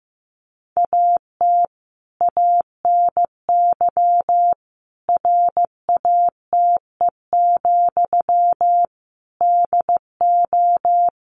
Morse code, heard as a keyed tone, about 14 wpm, 700 Hz. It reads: AT ANY RATE, DO
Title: Sentences from Top 1500 Words 15wpm